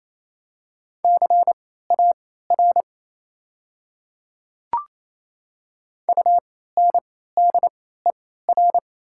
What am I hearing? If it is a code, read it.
Morse code, heard as a keyed tone, about 28 wpm, spaced as for 16 wpm, 700 Hz. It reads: CAR UNDER
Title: 3x Character Spacing - Top 300 Words - 28wpm